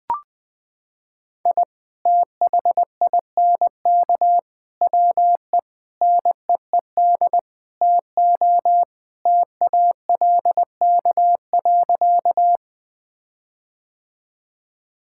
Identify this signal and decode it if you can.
Morse code, heard as a keyed tone, about 21 wpm, 700 Hz. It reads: I THINK WE NEED TO TALK.
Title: Sentences with 5-Word Phrases - 20wpm